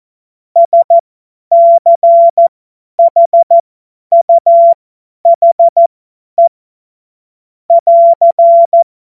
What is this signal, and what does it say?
Morse code, heard as a keyed tone, about 14 wpm, spaced as for 10 wpm, 680 Hz. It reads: SCHUHE <AR>
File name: Morsekurs / Witze / Witz 19. Oktober 2021, 50 F70